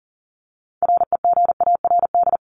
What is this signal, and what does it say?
Morse code, heard as a keyed tone, about 40 wpm, 700 Hz. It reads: REGARD
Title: ING Suffix - Top 100 Words with Base Words 40wpm